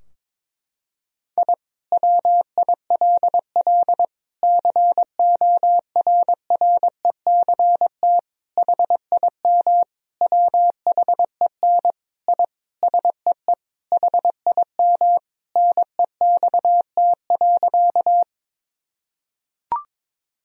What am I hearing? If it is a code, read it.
Morse code, heard as a keyed tone, about 22 wpm, 700 Hz. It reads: I WILL CORRECT HIM WHEN I SEE HIM NEXT.
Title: Sentences from Top 1000 Words - Review Part 2 - 22wpm